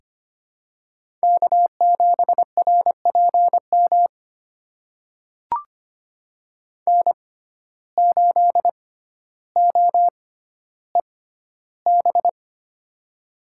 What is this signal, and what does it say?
Morse code, heard as a keyed tone, about 25 wpm, spaced as for 13 wpm, 700 Hz. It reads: K7RPM N8OEB
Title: USA Call Signs - 5 Characters - 6x Character Spacing - 25wpm